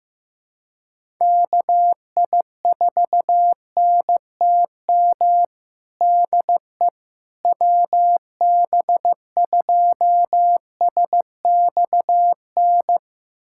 Morse code, heard as a keyed tone, about 15 wpm, 700 Hz. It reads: KI4NTM DE WB2SXN
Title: USA Callsign DE Callsign 15wpm